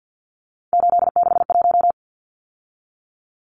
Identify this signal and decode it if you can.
Morse code, heard as a keyed tone, about 48 wpm, 700 Hz. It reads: 861
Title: Numbers - 3-Digit 50wpm